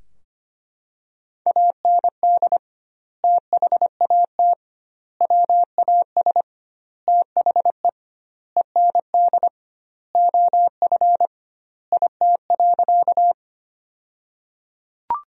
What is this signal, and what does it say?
Morse code, heard as a keyed tone, about 25 wpm, 700 Hz. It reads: AND THAT WAS THE END OF IT.
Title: 2x Word-Spacing - Sentences from Top 200 Words - 25wpm